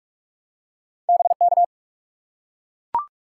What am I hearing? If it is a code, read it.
Morse code, heard as a keyed tone, about 44 wpm, 700 Hz. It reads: BK
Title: Top 180 QSO Elements 45wpm